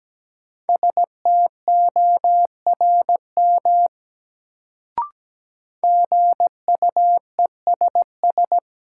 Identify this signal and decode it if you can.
Morse code, heard as a keyed tone, about 17 wpm, 700 Hz. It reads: STORM GUESS
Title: Sentences from Top 1500 Words - New Word Drill - 17wpm